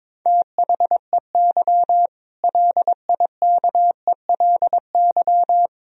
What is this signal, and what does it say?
Morse code, heard as a keyed tone, about 21 wpm, 700 Hz. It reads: THEY LIKELY
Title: Sentences from Top 1000 Words - Review Part 4 - 22wpm